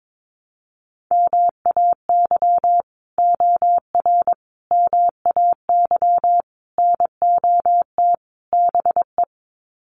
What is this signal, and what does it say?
Morse code, heard as a keyed tone, about 22 wpm, 700 Hz. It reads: MAY OR MAY NOT BE
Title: Sets of 5 Words 22wpm